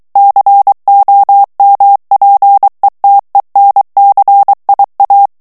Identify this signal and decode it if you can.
Morse code, heard as a keyed tone, about 23 wpm, 800 Hz. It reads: COMPETENCIA
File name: morse mp3 / ESPAGNOL TEXTE EN CLAIR 2 VITESSE 22 MOTS MINUTES